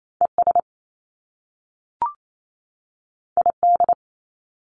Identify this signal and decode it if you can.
Morse code, heard as a keyed tone, about 28 wpm, 700 Hz. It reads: ES ID
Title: Top Two Letter Combinations in Words 28wpm